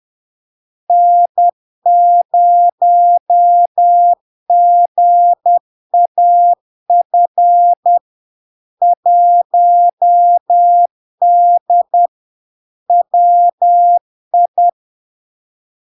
Morse code, heard as a keyed tone, about 10 wpm, 700 Hz. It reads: N0GAF 1D WI
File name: LARK-FD-2022-Log-in-CW-10wpm-700-Hz